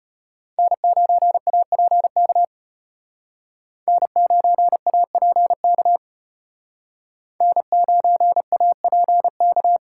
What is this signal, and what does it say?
Morse code, heard as a keyed tone, about 32 wpm, 700 Hz. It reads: N9APK N9APK N9APK